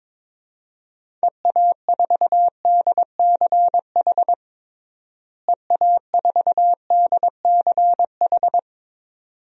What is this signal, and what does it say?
Morse code, heard as a keyed tone, about 22 wpm, 700 Hz. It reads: EA4DCH EA4DCH
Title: DXCC-100 Call Signs - 6 Characters - Repeated 2x - 22wpm